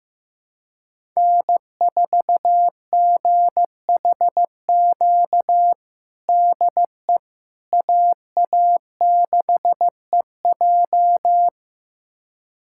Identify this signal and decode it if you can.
Morse code, heard as a keyed tone, about 15 wpm, 700 Hz. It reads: N4GHQ DE AA6EJ
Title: USA Callsign DE Callsign - 5 Characters - 15wpm